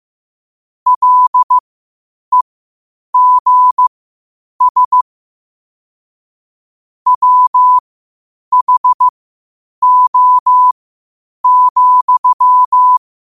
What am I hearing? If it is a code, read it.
Morse code, heard as a keyed tone, about 15 wpm, spaced as for 8 wpm, 1000 Hz. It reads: LEGS WHO,
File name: qotdfiles / 8wpm